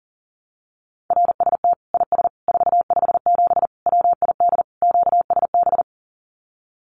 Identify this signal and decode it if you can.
Morse code, heard as a keyed tone, about 40 wpm, 700 Hz. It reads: RST IS 457 WID QSB
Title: Common QSO Phrases 40wpm